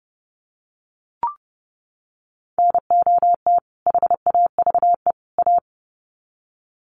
Morse code, heard as a keyed tone, about 30 wpm, 700 Hz. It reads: NOT HAVE A
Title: Sets of 3 Words 30wpm